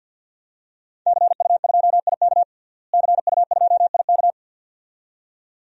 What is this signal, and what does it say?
Morse code, heard as a keyed tone, about 52 wpm, 700 Hz. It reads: CU2IK CU2IK
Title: DXCC-100 Call Signs - 5 Characters - Repeated 2x - 50wpm